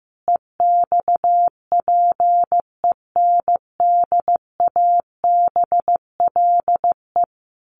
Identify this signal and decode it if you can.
Morse code, heard as a keyed tone, about 14 wpm, 700 Hz. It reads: EXPENDABLE
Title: A Sound Of Thunder 15wpm